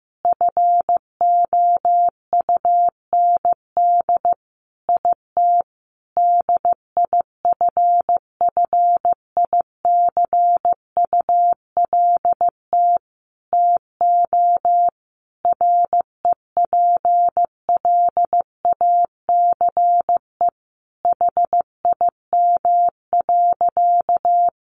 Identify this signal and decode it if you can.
Morse code, heard as a keyed tone, about 15 wpm, 700 Hz. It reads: FOUND IT DIFFICULT TO REPLACE HIM.
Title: Sentences from Top 1400 Words 15wpm